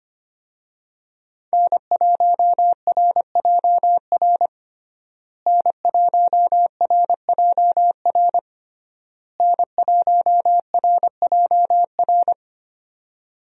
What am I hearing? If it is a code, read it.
Morse code, heard as a keyed tone, about 25 wpm, 700 Hz. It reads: N1RJR N1RJR N1RJR
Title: USA Call Signs - 5 Characters - Repeated 3x - 25wpm